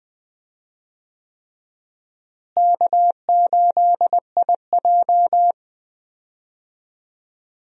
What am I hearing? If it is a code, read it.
Morse code, heard as a keyed tone, about 21 wpm, 700 Hz. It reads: K8IJ
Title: USA Call Signs - 4 Characters - Repeated 3x - 20wpm